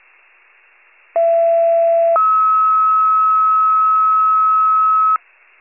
Two-tone page, heard as a one-second tone, then a three-second tone, about 670 then 1290 Hz. It is heard from 1.2 to 5.2 s.